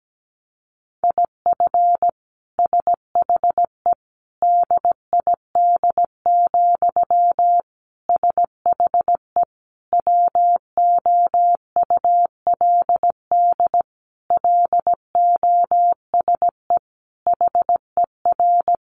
Morse code, heard as a keyed tone, about 17 wpm, 700 Hz. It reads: IF SHE DID, SHE WOULD LOSE HER
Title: Sentences from Top 1400 Words 17wpm